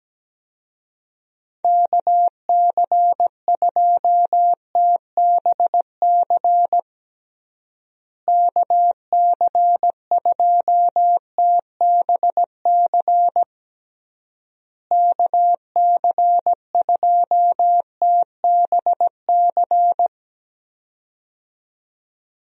Morse code, heard as a keyed tone, about 17 wpm, 700 Hz. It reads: KC2TBC KC2TBC KC2TBC
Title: USA Call Signs - 6 Characters - Repeated 3x - 17wpm